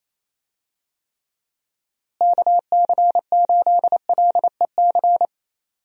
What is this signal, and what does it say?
Morse code, heard as a keyed tone, about 27 wpm, 700 Hz. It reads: KC8LEC
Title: USA Call Signs - 6 Characters - Repeated 3x - 28wpm